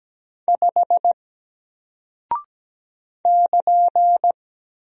Morse code, heard as a keyed tone, about 17 wpm, 700 Hz. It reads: 5 <KN>
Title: Single Character (Letter:Number:Punctuation:Prosign) Alt 17wpm